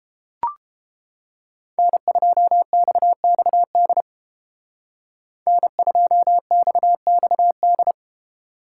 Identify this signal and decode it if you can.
Morse code, heard as a keyed tone, about 28 wpm, 700 Hz. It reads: N2XXD N2XXD